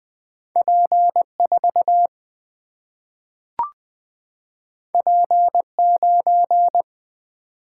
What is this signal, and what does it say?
Morse code, heard as a keyed tone, about 20 wpm, 700 Hz. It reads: P4 P9
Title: Single-Letter Single-Number 20wpm